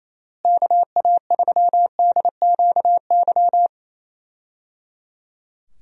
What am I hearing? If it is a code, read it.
Morse code, heard as a keyed tone, about 28 wpm, 700 Hz. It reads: KA3DQY
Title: USA Call Signs - 6 Characters - Repeated 3x - 28wpm